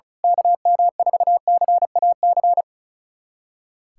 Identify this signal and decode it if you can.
Morse code, heard as a keyed tone, about 36 wpm, 700 Hz. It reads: KM4CAC
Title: USA Callsign DE Callsign 35wpm